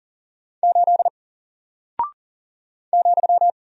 Morse code, heard as a keyed tone, about 40 wpm, 700 Hz. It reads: 8 ,
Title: Single Character (Letter:Number:Punctuation:Prosign) Alt 40wpm